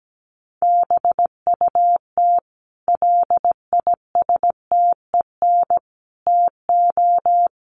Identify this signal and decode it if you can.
Morse code, heard as a keyed tone, about 17 wpm, 700 Hz. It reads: BUT LISTEN TO
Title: Sentences from Top 400 Words 17wpm